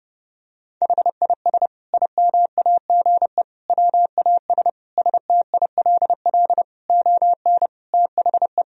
Morse code, heard as a keyed tone, about 30 wpm, 700 Hz. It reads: HIS IMAGE WAS STILL ON THE
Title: Sentences from Top 1000 Words - Review Part 4 - 30wpm